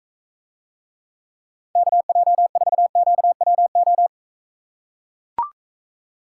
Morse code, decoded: KJ4QWO

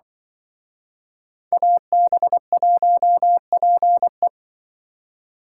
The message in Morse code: AB1PE

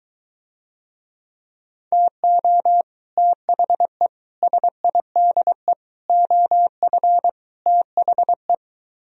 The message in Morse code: TO THE SIDE OF THE

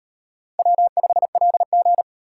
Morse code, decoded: W5LG

